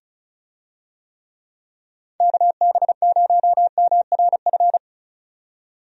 Morse code, heard as a keyed tone, about 34 wpm, 700 Hz. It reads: KB0MRF
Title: USA Call Signs - 6 Characters - 35wpm